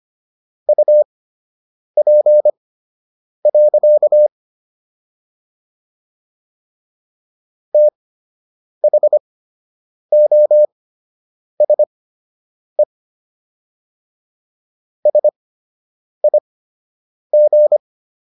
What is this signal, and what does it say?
Morse code, heard as a keyed tone, about 25 wpm, spaced as for 8 wpm, 600 Hz. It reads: UP. THOSE SIG